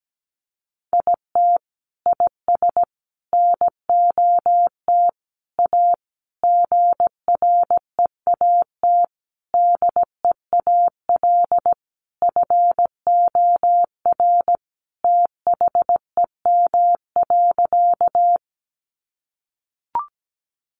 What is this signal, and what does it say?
Morse code, heard as a keyed tone, about 17 wpm, 700 Hz. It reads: IT IS NOT A GREAT DEAL FOR THEM.
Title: Sentences from Top 800 Words 17wpm